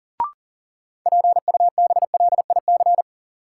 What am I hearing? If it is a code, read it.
Morse code, heard as a keyed tone, about 40 wpm, 700 Hz. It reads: PUBLIC
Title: Sets of 2 Words - New Word Drill - 40wpm